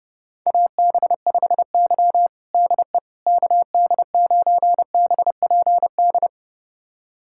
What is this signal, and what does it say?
Morse code, heard as a keyed tone, about 30 wpm, 700 Hz. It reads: AB5Y DE KD9BPD
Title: USA Callsign DE Callsign 30wpm